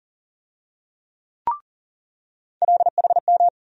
Morse code, heard as a keyed tone, about 40 wpm, 700 Hz. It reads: LHM
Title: USA Call Sign Postfixes - 3 Letters 40wpm